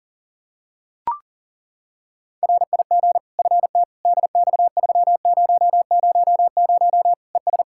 Morse code, heard as a keyed tone, about 40 wpm, 700 Hz. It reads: RIG FT DX3000 ES
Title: K1USN SST QSO Practice 40wpm